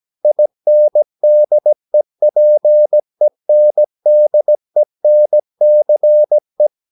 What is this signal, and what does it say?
Morse code, heard as a keyed tone, about 17 wpm, 600 Hz. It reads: INDEPENDENCE
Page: Romanen Never af Ken Follet på engelsk.